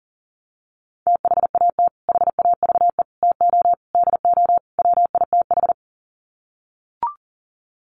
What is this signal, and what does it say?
Morse code, heard as a keyed tone, about 37 wpm, 700 Hz. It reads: THAT HAVE TO DO WITH
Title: Sets of 5 Words Encore2 40wpm